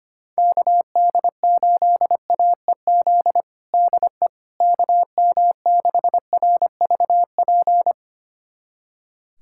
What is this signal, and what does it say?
Morse code, heard as a keyed tone, about 25 wpm, 700 Hz. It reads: KD8AEZ DE KM6RVP
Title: USA Callsign DE Callsign - 6 Characters - 25wpm